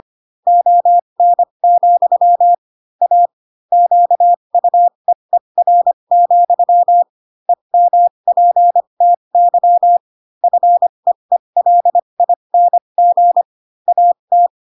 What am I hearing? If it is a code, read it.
Morse code, heard as a keyed tone, about 25 wpm, spaced as for 22 wpm, 700 Hz. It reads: ON, A QUEER, EMPTY FEELING AT